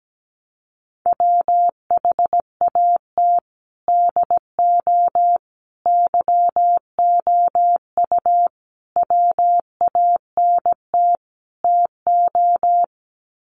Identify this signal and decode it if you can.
Morse code, heard as a keyed tone, about 17 wpm, 700 Hz. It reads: WHAT DO YOU WANT TO
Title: Sets of 5 Words 17wpm